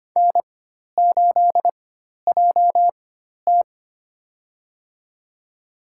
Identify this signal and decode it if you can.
Morse code, heard as a keyed tone, about 25 wpm, spaced as for 12 wpm, 700 Hz. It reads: N8JT
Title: USA Call Signs - 4 Characters - 4x Character Spacing - 25wpm